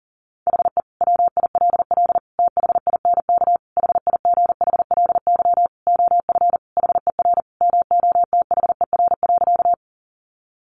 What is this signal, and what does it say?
Morse code, heard as a keyed tone, about 37 wpm, 700 Hz. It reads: HE WILL THINK HIGHLY OF HER MOTHER.
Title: Sentences from Top 1100 Words 40wpm